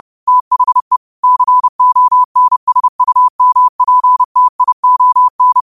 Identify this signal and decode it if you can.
Morse code, heard as a keyed tone, about 30 wpm, 1000 Hz. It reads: THE CONSUMPTION
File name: qotdfiles / 30wpm